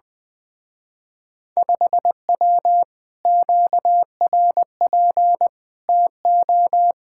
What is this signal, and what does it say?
Morse code, heard as a keyed tone, about 20 wpm, 700 Hz. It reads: 5W QRP TO
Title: Speed-Racing Common QSO Phrases - 20wpm